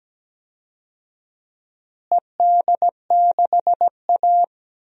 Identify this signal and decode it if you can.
Morse code, heard as a keyed tone, about 17 wpm, 700 Hz. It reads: ED6A
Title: DXCC-100 Call Signs - 4 Characters - Repeated 3x - 17wpm